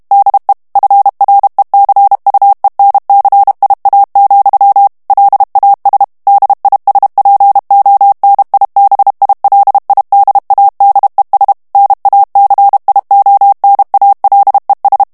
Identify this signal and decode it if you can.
Morse code, heard as a keyed tone, about 32 wpm, 800 Hz. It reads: DE FRECUENCIA, LAS DISPONIBILIDADES NACIONALES
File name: ESPAGNOL TEXTE EN CLAIR 2 VITESSE 30 MOTS MINUTES